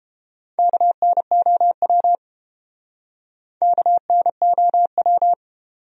Morse code, heard as a keyed tone, about 32 wpm, 700 Hz. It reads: KNOW KNOW